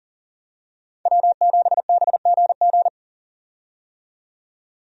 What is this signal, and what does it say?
Morse code, heard as a keyed tone, about 40 wpm, 700 Hz. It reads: W7BGG
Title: USA Call Signs - 5 Characters - 40wpm